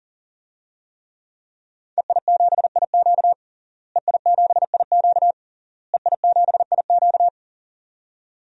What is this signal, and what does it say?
Morse code, heard as a keyed tone, about 40 wpm, 700 Hz. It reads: EI7IQ EI7IQ EI7IQ
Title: DXCC-100 Call Signs - 5 Characters - Repeated 3x - 40wpm